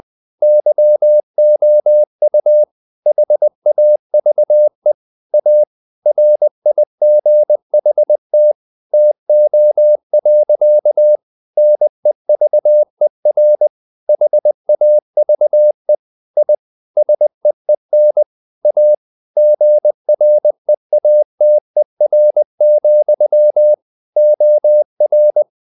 Morse code, heard as a keyed tone, about 20 wpm, 600 Hz. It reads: YOU HAVE A RIGHT TO. NEVER HAVE I SEEN A GREATER, OR
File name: the old man and the sea part 2 20wpm